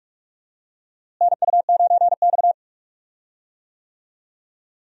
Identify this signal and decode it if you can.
Morse code, heard as a keyed tone, about 45 wpm, 700 Hz. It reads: NU9X
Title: USA Call Signs - 4 Characters - Repeated 3x - 45wpm